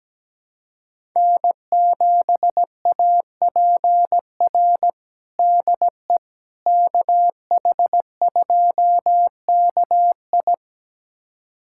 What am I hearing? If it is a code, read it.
Morse code, heard as a keyed tone, about 17 wpm, 700 Hz. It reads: N7APR DE KH2KI
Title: USA Callsign DE Callsign - 5 Characters - 17wpm